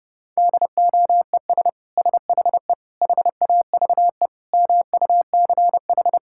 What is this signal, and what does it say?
Morse code, heard as a keyed tone, about 30 wpm, 700 Hz. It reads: DOES SHE HAVE MUCH
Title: Sentences from Top 1000 Words - Review Part 6 - 30wpm